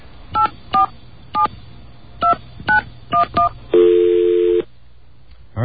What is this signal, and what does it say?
DTMF, heard as telephone touch tones, tones of about 120 ms, with speech. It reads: 8472911